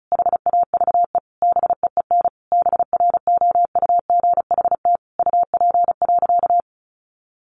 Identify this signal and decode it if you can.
Morse code, heard as a keyed tone, about 34 wpm, 700 Hz. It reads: HAVE BEEN BROUGHT UP.
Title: Sentences from Top 1500 Words 35wpm